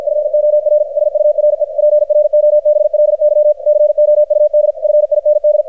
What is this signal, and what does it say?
Morse code, heard as a keyed tone, about 53 wpm, 600 Hz, with radio noise. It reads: FOR SURE PROBLY COUD FIND
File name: archived / CRSnet